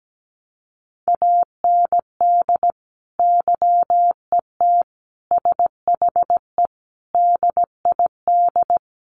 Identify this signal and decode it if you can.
Morse code, heard as a keyed tone, about 17 wpm, 700 Hz. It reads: AND YET SHE DID
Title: Sentences from Top 1500 Words 17wpm